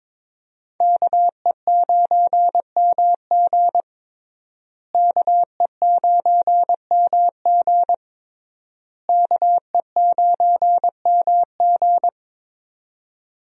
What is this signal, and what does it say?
Morse code, heard as a keyed tone, about 22 wpm, 700 Hz. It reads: KE9MG KE9MG KE9MG
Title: USA Call Signs - 5 Characters - Repeated 3x - 22wpm